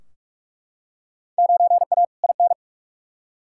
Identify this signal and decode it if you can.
Morse code, heard as a keyed tone, about 42 wpm, 700 Hz. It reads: 9A IN
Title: Field Day Exchanges Encore - 45wpm